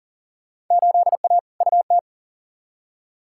Morse code, heard as a keyed tone, about 40 wpm, 700 Hz. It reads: 8A UT
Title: Field Day Exchanges Encore - 40wpm